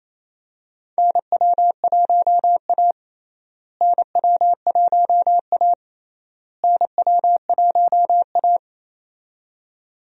Morse code, heard as a keyed tone, about 27 wpm, 700 Hz. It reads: NW1A NW1A NW1A
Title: USA Call Signs - 4 Characters - Repeated 3x - 28wpm